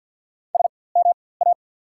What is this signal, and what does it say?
Morse code, heard as a keyed tone, about 48 wpm, spaced as for 23 wpm, 690 Hz. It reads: SMA
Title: 4x Character Spacing - Top 300 Words - 50wpm